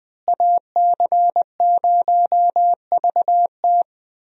Morse code, heard as a keyed tone, about 20 wpm, 700 Hz. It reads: AC0VT
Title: USA Call Signs - 5 Characters - Repeated 2x - 20wpm